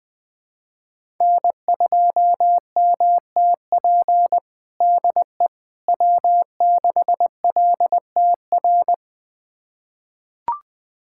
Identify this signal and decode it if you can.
Morse code, heard as a keyed tone, about 20 wpm, 700 Hz. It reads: N2MTP DE W6LTR
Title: USA Callsign DE Callsign - 5 Characters - 20wpm